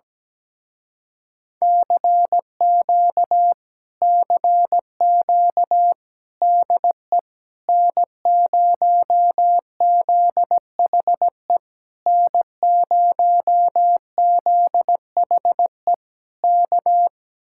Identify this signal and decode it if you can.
Morse code, heard as a keyed tone, about 17 wpm, 700 Hz. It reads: CQ CQ DE N0ZHE N0ZHE K
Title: Speed-Racing Common QSO Phrases - 17wpm